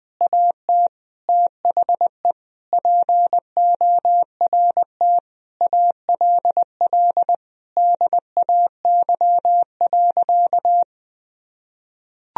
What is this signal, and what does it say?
Morse code, heard as a keyed tone, about 20 wpm, 700 Hz. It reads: AT THE PORT ALL DAY.
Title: Sentences from Top 1000 Words - Review Part 1 - 20wpm